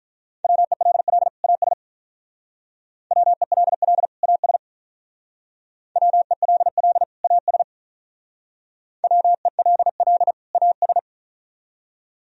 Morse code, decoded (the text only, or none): WELL AS WELL AS WELL AS WELL AS